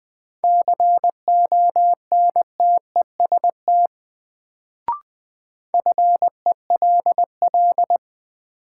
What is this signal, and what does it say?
Morse code, heard as a keyed tone, about 21 wpm, 700 Hz. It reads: CONTEST FELL
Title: Sentences from Top 1600 Words - New Word Drill Alt - 20wpm